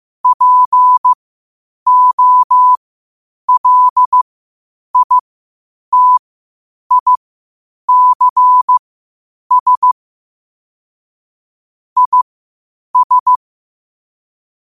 Morse code, heard as a keyed tone, about 15 wpm, spaced as for 8 wpm, 1000 Hz. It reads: POLITICS IS
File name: qotdfiles / 8wpm